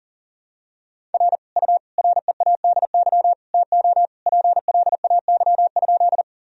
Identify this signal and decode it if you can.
Morse code, heard as a keyed tone, about 40 wpm, 700 Hz. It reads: R U READY TO PLAY?
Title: Sentences from Top 400 Words and Abbreviations 40wpm